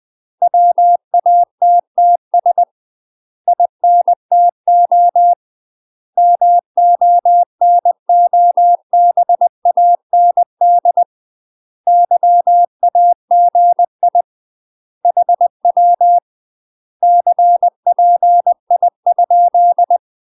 Morse code, heard as a keyed tone, about 20 wpm, 700 Hz. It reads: WATTS INTO MONOBAND YAGI HW CPI?
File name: UK MASTER QSOs TRAINING / 20WPM / UK MASTER QSOs 0068